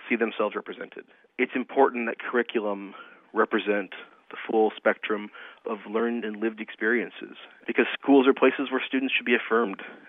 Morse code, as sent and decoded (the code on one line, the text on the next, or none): none